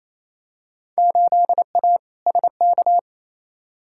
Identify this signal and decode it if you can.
Morse code, heard as a keyed tone, about 28 wpm, 700 Hz. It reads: 8A SK
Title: Field Day Exchanges Encore - 28wpm